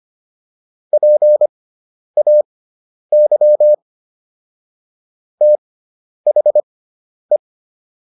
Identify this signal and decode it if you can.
Morse code, heard as a keyed tone, about 25 wpm, spaced as for 10 wpm, 600 Hz. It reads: PAY THE